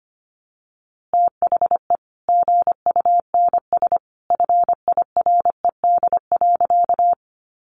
Morse code, heard as a keyed tone, about 25 wpm, 700 Hz. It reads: THE GUNS FIRED.
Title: A Sound Of Thunder 25wpm